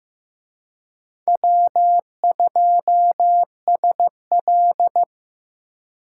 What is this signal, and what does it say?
Morse code, heard as a keyed tone, about 18 wpm, 700 Hz. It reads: W2SL
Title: Speed Racing - USA Call Signs 15wpm